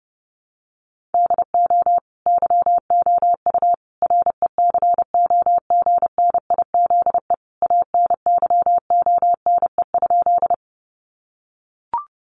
Morse code, heard as a keyed tone, about 30 wpm, 700 Hz. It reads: DO YOU RECOGNIZE ANYONE?
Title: Sentences from Top 1400 Words 30wpm